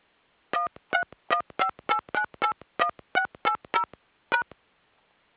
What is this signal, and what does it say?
DTMF, heard as telephone touch tones, tones of about 90 ms. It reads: 13158981680#